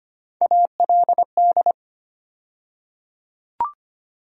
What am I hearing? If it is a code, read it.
Morse code, heard as a keyed tone, about 25 wpm, 700 Hz. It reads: ALD
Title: USA Call Sign Postfixes - 3 Letters 25wpm